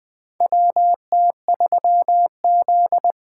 Morse code, heard as a keyed tone, about 20 wpm, 700 Hz. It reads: WT3Z
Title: USA Call Signs - 4 Characters - 6x Character Spacing - 20wpm